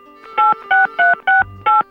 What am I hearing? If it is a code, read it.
DTMF, heard as telephone touch tones, tones of about 170 ms. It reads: 86368